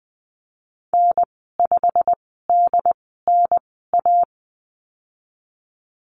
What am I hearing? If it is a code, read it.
Morse code, heard as a keyed tone, about 20 wpm, spaced as for 14 wpm, 700 Hz. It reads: N5DNA
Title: 193 Lesson 31g - Letter v - Call Signs 20wpm